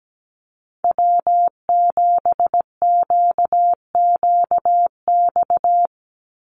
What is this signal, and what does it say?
Morse code, heard as a keyed tone, about 17 wpm, 700 Hz. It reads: W7QQX
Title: USA Call Signs 17wpm